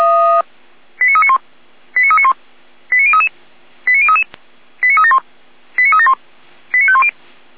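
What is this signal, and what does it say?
Five-tone selective call (ZVEI), heard as quick runs of five tones, tones of about 70 ms. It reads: 89391 89391 89033 89033 89371 89371 89510
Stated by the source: ZVEI five-tone selective call received on an SDR.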